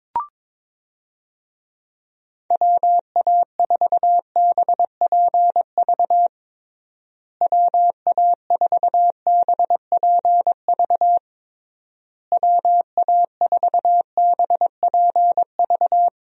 Morse code, heard as a keyed tone, about 22 wpm, 700 Hz. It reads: WA4BPV WA4BPV WA4BPV
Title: USA Call Signs - 6 Characters - Repeated 3x - 22wpm